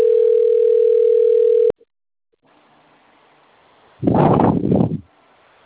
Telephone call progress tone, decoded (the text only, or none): ringback tone